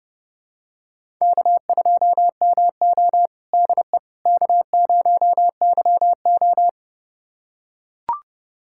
Morse code, heard as a keyed tone, about 30 wpm, 700 Hz. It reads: K2MO DE K0YO
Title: USA Callsign DE Callsign - 4 Characters - 30wpm